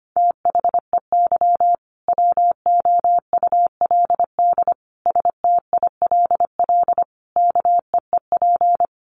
Morse code, heard as a keyed tone, about 25 wpm, 700 Hz. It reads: THEY WOULD STILL KEEP
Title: Sentences from Top 1400 Words 25wpm